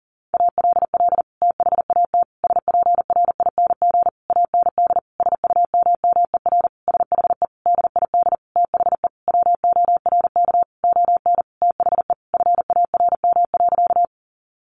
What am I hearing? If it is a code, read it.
Morse code, heard as a keyed tone, about 40 wpm, 700 Hz. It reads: ALL THAT SPRING AND SUMMER SHE DID THE WORK ON THE FARM.
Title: Sentences from Top 1200 Words 40wpm